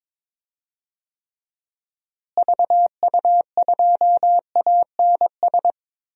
Morse code, heard as a keyed tone, about 22 wpm, 700 Hz. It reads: VU2ANS
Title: DXCC-100 Call Signs - 6 Characters - Repeated 3x - 22wpm